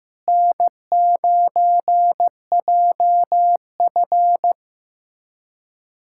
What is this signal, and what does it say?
Morse code, heard as a keyed tone, about 15 wpm, 700 Hz. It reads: N9JF
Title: K1USN SST QSO Practice - 15wpm